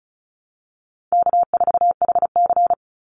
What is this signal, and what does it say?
Morse code, heard as a keyed tone, about 34 wpm, 700 Hz. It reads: K4HC
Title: Easy Call Signs Repeated 2x - 35wpm